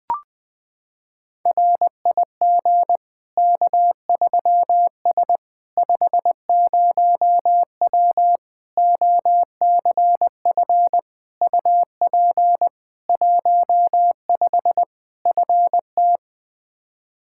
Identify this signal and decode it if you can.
Morse code, heard as a keyed tone, about 21 wpm, 700 Hz. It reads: RIG K3S 50W OCF UP 15 FT
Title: K1USN SST QSO Practice 20wpm